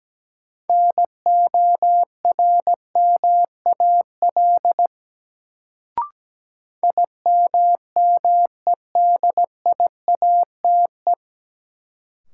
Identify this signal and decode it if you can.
Morse code, heard as a keyed tone, about 17 wpm, 700 Hz. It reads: NORMAL IMMEDIATE
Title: Sentences from Top 1200 Words - New Word Drill - 17wpm